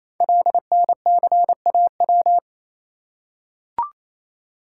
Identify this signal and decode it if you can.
Morse code, heard as a keyed tone, about 29 wpm, 700 Hz. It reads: LNCAW
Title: Code Groups - 5 Letters - 28wpm